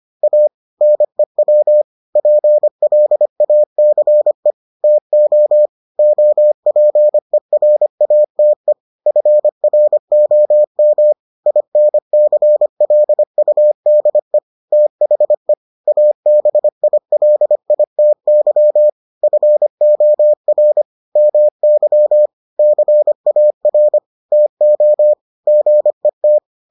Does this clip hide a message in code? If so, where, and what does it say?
Yes: Morse code from 0.2 to 26.4 s, heard as a keyed tone, about 25 wpm, 600 Hz. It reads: A NEW PLACE TO OPERATE FROM INCLUDE THE ABILITY FOR MY CAR TO GET